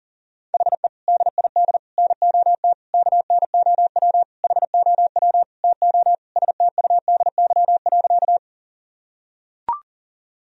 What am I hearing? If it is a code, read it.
Morse code, heard as a keyed tone, about 37 wpm, 700 Hz. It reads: HE DID NOT KNOW HOW TO STUDY.
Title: Sentences from Top 1000 Words - Review Part 1 - 40wpm